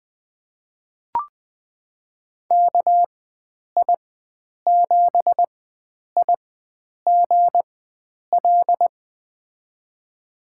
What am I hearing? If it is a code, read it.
Morse code, heard as a keyed tone, about 20 wpm, spaced as for 9 wpm, 700 Hz. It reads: KI7IGL
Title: USA Call Signs - 6 Characters - 4x Character Spacing - 20wpm